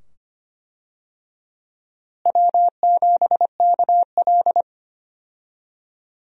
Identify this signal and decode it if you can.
Morse code, heard as a keyed tone, about 25 wpm, 700 Hz. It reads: W7KL
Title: USA Call Signs - 4 Characters - Repeated 2x - 25wpm